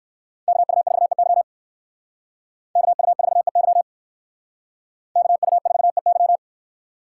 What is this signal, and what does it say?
Morse code, heard as a keyed tone, about 58 wpm, 700 Hz. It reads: KU4EY KU4EY KU4EY